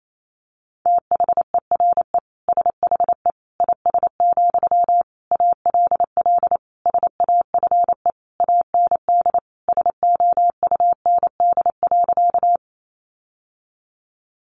Morse code, heard as a keyed tone, about 28 wpm, 700 Hz. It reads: THERE SHE IS, ALL SAFE AND SOUND.
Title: Sentences from Top 900 Words 28wpm